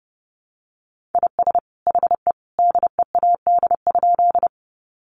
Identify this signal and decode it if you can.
Morse code, heard as a keyed tone, about 30 wpm, 700 Hz. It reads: IS HE DEAD?
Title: Sentences from Top 1500 Words 30wpm